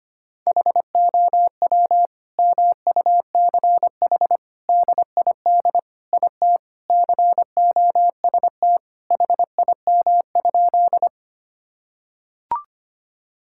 Morse code, heard as a keyed tone, about 25 wpm, 700 Hz. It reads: HOW MUCH DID IT COST HIM?
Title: Sentences from Top 1000 Words - Review Part 5 - 25wpm